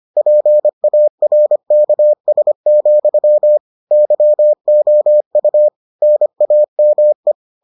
Morse code, heard as a keyed tone, about 25 wpm, 600 Hz. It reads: PARKS, YOU NAME